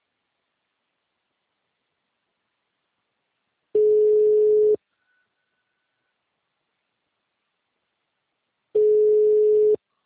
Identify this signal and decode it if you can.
Ringback tone (Europe, 425 Hz).
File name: out- -2015- - - .629